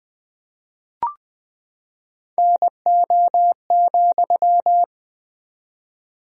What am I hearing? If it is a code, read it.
Morse code, heard as a keyed tone, about 19 wpm, 700 Hz. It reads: NO,
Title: 4x Word-Spacing - Sentences from Top 300 Words - 20wpm